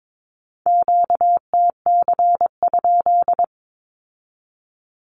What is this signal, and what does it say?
Morse code, heard as a keyed tone, about 22 wpm, 700 Hz. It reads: QTC?
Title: Q-Codes with Spoken Meaning - 22wpm